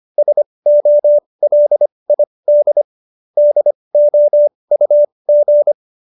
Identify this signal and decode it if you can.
Morse code, heard as a keyed tone, about 25 wpm, spaced as for 20 wpm, 570 Hz. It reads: SOLID DOUG
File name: qso220b 20